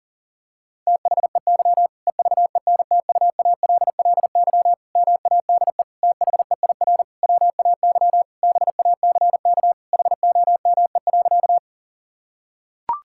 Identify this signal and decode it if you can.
Morse code, heard as a keyed tone, about 40 wpm, 700 Hz. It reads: THEY EVENTUALLY MADE THEIR WAY BACK HOME.
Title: Sentences from Top 1000 Words - Review Part 6 - 40wpm